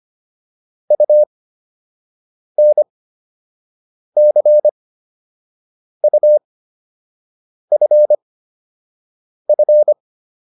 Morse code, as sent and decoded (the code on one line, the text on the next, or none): ..- -. -.-. ..- ..-. ..-.
UNCUFF